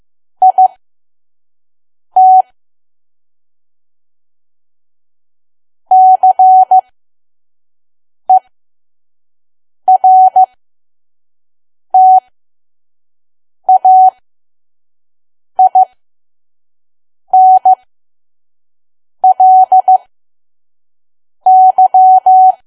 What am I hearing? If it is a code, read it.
Morse code, heard as a keyed tone, about 15 wpm, spaced as for 5 wpm, 750 Hz. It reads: IT CERTAINLY
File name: Archive / 5 WPM files / PM